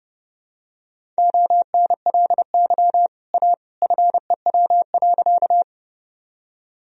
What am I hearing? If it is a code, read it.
Morse code, heard as a keyed tone, about 30 wpm, 700 Hz. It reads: ONLY A FEW.
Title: Sentences from Top 1000 Words - Review Part 1 - 30wpm